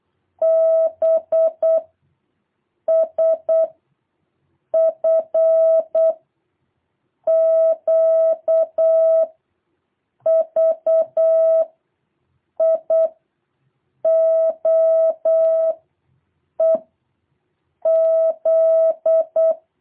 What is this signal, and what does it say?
Morse code, heard as a keyed tone, about 8 wpm, 660 Hz. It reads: B S F Q V I O E Z
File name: jh8cbh / 2023cw